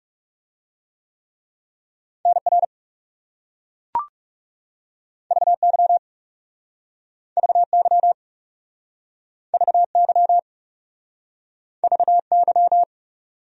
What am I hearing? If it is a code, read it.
Morse code, heard as a keyed tone, about 40 wpm, 700 Hz. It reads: NR VY VY VY VY